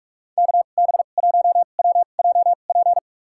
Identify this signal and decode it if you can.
Morse code, heard as a keyed tone, about 46 wpm, spaced as for 32 wpm, 700 Hz. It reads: KB1WJP
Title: USA Call Signs - 2x Character Spacing - 45wpm